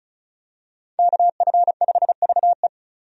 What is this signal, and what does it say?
Morse code, heard as a keyed tone, about 36 wpm, 700 Hz. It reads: KF5VE
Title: USA Call Signs - 5 Characters - Repeated 2x - 35wpm